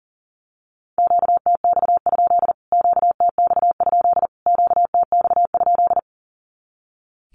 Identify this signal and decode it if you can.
Morse code, heard as a keyed tone, about 40 wpm, 700 Hz. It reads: QTX? QTX? QTX?
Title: Q-Codes Repeated 3x 40wpm